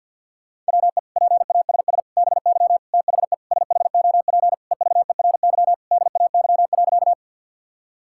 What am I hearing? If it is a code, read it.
Morse code, heard as a keyed tone, about 51 wpm, 700 Hz. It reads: WE PASS BY THE SHOP EVERY DAY.
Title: Sentences from Top 1600 Words 50wpm